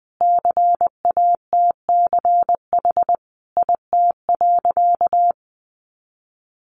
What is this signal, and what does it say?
Morse code, heard as a keyed tone, about 20 wpm, 700 Hz. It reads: CATCH IT.
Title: Additional Easy Phrases 20wpm